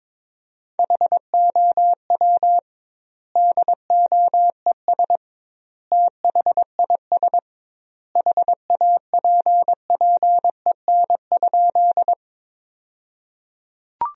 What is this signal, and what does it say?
Morse code, heard as a keyed tone, about 22 wpm, 700 Hz. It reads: HOW DOES THIS HAPPEN?
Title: 2x Word-Spacing - Sentences from Top 400 Words - 22wpm